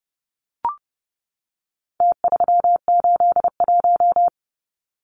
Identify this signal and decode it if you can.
Morse code, heard as a keyed tone, about 29 wpm, 700 Hz. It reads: T381
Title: Numbers - Contest Style - 4-Digit 30wpm